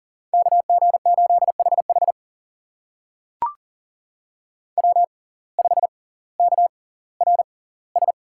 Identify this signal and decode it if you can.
Morse code, heard as a keyed tone, about 39 wpm, spaced as for 20 wpm, 690 Hz. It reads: KG8HH W5KRS
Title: USA Call Signs - 5 Characters - 6x Character Spacing - 40wpm